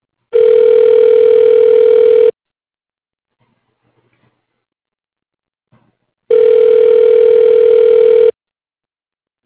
Ringback tone (North America), 440 + 480 Hz.